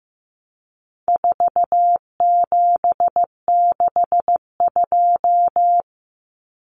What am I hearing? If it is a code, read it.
Morse code, heard as a keyed tone, about 15 wpm, 700 Hz. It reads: 4762
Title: Numbers - 4-Digit 15wpm